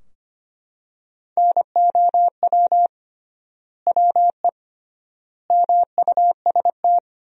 Morse code, heard as a keyed tone, about 25 wpm, 700 Hz. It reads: NOW WE MUST
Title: 3x Word-Spacing - Sentences from Top 500 Words - 25wpm